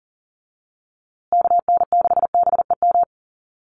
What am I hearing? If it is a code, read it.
Morse code, heard as a keyed tone, about 40 wpm, 700 Hz. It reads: KN6BEM
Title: USA Call Signs Repeated 3x 40wpm